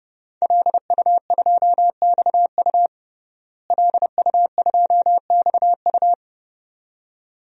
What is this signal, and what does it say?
Morse code, heard as a keyed tone, about 30 wpm, 700 Hz. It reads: LU2XU LU2XU
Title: DXCC-100 Call Signs - 5 Characters - Repeated 2x - 30wpm